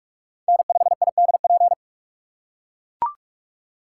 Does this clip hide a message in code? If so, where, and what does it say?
Yes: Morse code from 0.5 to 1.7 s, heard as a keyed tone, about 45 wpm, 700 Hz. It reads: N5IDP